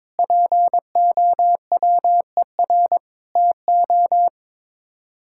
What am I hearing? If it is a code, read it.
Morse code, heard as a keyed tone, about 22 wpm, 700 Hz. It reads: POWER TO
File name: practice / 22wpm / Sets of 8 Words - 22wpm